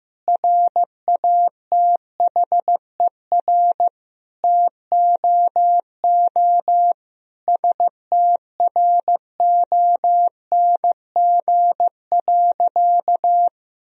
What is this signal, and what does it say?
Morse code, heard as a keyed tone, about 16 wpm, 700 Hz. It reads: RATHER TOO STRONG.
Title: Sentences from Top 1000 Words - Review Part 4 - 15wpm